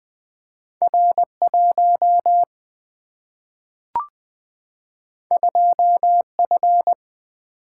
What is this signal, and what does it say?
Morse code, heard as a keyed tone, about 20 wpm, 700 Hz. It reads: R 1 2 F
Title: Single-Letter Single-Number 20wpm